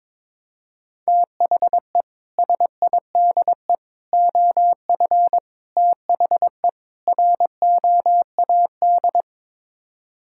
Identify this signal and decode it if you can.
Morse code, heard as a keyed tone, about 22 wpm, 700 Hz. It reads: THE SIDE OF THE ROAD
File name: practice / 22wpm / Build A Phrase - 5-Word Phrases - 22wpm